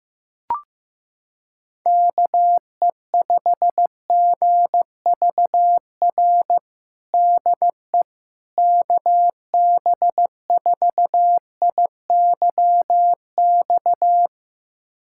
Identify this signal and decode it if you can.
Morse code, heard as a keyed tone, about 16 wpm, 700 Hz. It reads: KE5GVR DE KB4IYX
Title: USA Callsign DE Callsign 15wpm